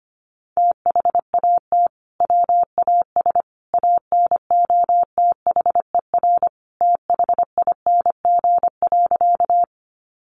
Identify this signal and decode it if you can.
Morse code, heard as a keyed tone, about 25 wpm, 700 Hz. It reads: THAT WAS ANOTHER THING.
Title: Sentences from Top 600 Words 25wpm